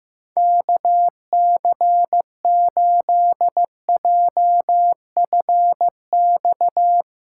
Morse code, heard as a keyed tone, about 16 wpm, 700 Hz. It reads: KC8JFX
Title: USA Call Signs - 4x Character Spacing - 15wpm